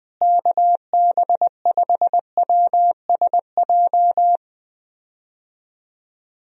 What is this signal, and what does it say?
Morse code, heard as a keyed tone, about 21 wpm, 700 Hz. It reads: KB5WSJ
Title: USA Call Signs - 6 Characters - Repeated 3x - 20wpm